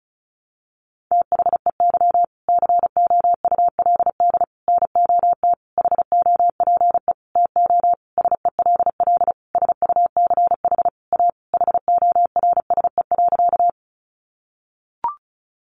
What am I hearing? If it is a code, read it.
Morse code, heard as a keyed tone, about 34 wpm, 700 Hz. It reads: THEY COULD NOT HOPE TO SELL SUCH A HORSE.
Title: Sentences from Top 1100 Words 35wpm